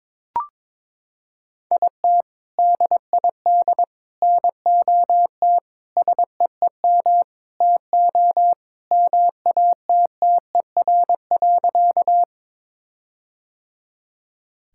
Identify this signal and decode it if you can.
Morse code, heard as a keyed tone, about 22 wpm, 700 Hz. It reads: IT DID NOT SEEM TO MATTER.
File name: practice / 22wpm / Sentences with 4-Word Phrases - 22wpm